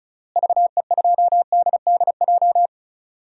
Morse code, heard as a keyed tone, about 36 wpm, 700 Hz. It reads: VE2DDJ
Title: DXCC-100 Call Signs - 6 Characters - Repeated 3x - 35wpm